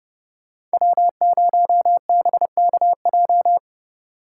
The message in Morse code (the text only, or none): W0BKJ